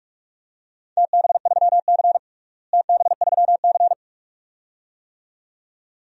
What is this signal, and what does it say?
Morse code, heard as a keyed tone, about 46 wpm, 700 Hz. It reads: TB3C TB3C
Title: DXCC-100 Call Signs - 4 Characters - Repeated 2x - 45wpm